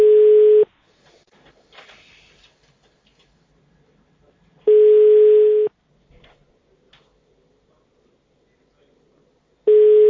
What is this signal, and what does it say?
Ringback tone (Europe, 425 Hz).